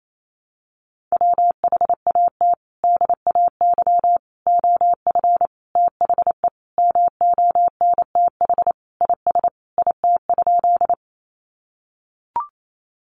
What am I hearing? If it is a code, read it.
Morse code, heard as a keyed tone, about 28 wpm, 700 Hz. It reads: WHAT DAY OF THE MONTH IS IT?
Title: Sentences from Top 800 Words 28wpm